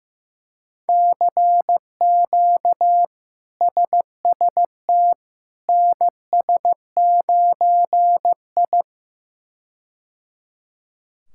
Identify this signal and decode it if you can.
Morse code, heard as a keyed tone, about 15 wpm, 700 Hz. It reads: CQ SST NS9I
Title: K1USN SST QSO Practice - 15wpm